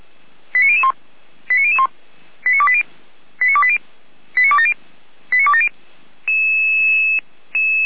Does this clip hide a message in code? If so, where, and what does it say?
Yes: five-tone selective call (ZVEI) from 0.5 to 5.7 s, heard as quick runs of five tones, tones of about 70 ms. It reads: 89001 89001 89280 89280 89270 89270